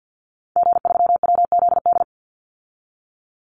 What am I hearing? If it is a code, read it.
Morse code, heard as a keyed tone, about 48 wpm, 700 Hz. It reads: G3WZD
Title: Call Signs Repeated 2x - 50wpm